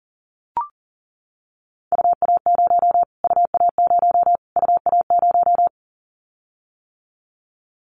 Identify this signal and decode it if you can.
Morse code, heard as a keyed tone, about 40 wpm, 700 Hz. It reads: UA0 UA0 UA0
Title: Callsign Prefixes for Top 25 DXCC - Repeated 3x - 40wpm